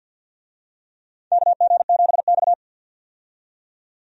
Morse code, heard as a keyed tone, about 53 wpm, 700 Hz. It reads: KG7X
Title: USA Call Signs - 4 Characters - 50wpm